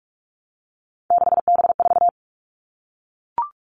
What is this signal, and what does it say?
Morse code, heard as a keyed tone, about 44 wpm, 700 Hz. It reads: 6B4
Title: All Cut Numbers - 3-Digit 45wpm